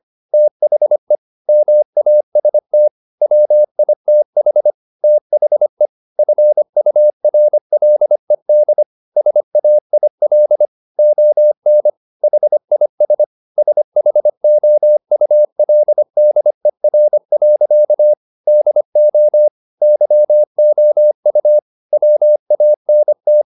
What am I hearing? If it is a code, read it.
Morse code, heard as a keyed tone, about 25 wpm, 600 Hz. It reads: THE MAST WITH THE FURLED SAIL ON HIS SHOULDER. DO YOU WANT
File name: the old man and the sea part 1 25wpm